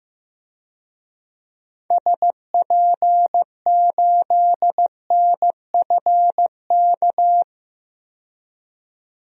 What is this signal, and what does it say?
Morse code, heard as a keyed tone, about 16 wpm, 700 Hz. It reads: SP8NFK
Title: DXCC-100 Call Signs - 6 Characters - 15wpm